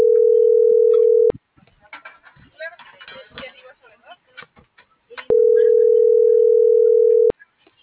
Ringback tone (North America), 440 + 480 Hz.